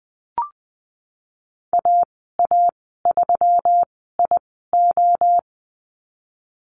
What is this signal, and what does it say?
Morse code, heard as a keyed tone, about 20 wpm, spaced as for 15 wpm, 700 Hz. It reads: AA3IO
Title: 167 Lesson 27g - Number 7 - Call Signs 20wpm